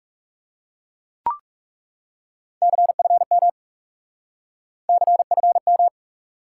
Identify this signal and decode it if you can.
Morse code, heard as a keyed tone, about 40 wpm, 700 Hz. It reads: CFM CFM